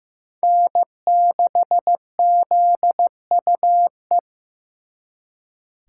Morse code, heard as a keyed tone, about 18 wpm, 700 Hz. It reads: N6ZUE
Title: Speed Racing - USA Call Signs 15wpm